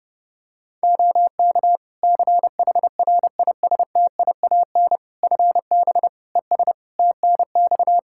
Morse code, heard as a keyed tone, about 30 wpm, 700 Hz. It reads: OK CHRISTIAN FB ES TNX
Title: Practice QSO - Exchange 2 - Info - 30wpm